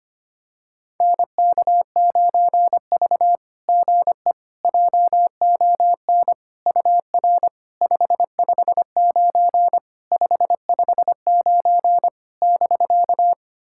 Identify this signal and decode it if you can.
Morse code, heard as a keyed tone, about 25 wpm, 700 Hz. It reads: NK9V GE JON UR 559 559 <BK>
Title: SOTA QSO Practice 25wpm